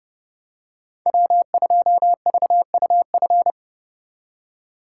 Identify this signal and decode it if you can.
Morse code, heard as a keyed tone, about 31 wpm, 700 Hz. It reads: W2VUF
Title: K1USN SST QSO Practice - 30wpm